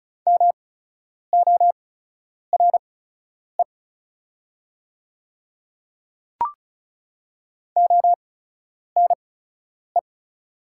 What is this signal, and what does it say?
Morse code, heard as a keyed tone, about 36 wpm, spaced as for 10 wpm, 700 Hz. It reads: MORE ONE
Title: 8x Character Spacing - Top 200 Words - 35wpm